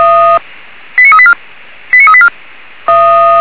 Five-tone selective call (ZVEI), heard as quick runs of five tones, tones of about 70 ms. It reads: 89384 89384